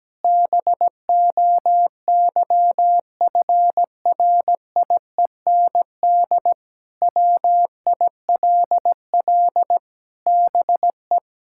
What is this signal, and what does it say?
Morse code, heard as a keyed tone, about 17 wpm, 700 Hz. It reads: BOYFRIEND WILL BE
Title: Sentences from Top 1000 Words - Review Part 5 - 17wpm